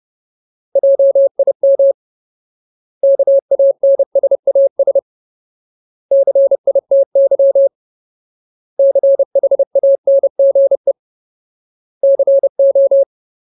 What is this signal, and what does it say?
Morse code, heard as a keyed tone, about 30 wpm, 550 Hz. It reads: JIM KANSAS CITY CHANGE CO
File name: Ultimate Word List / 30WPM Ultimate